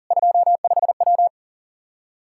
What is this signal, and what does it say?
Morse code, heard as a keyed tone, about 37 wpm, 700 Hz. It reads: 25W
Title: Practice QSO - Exchange 2 - Info - 40wpm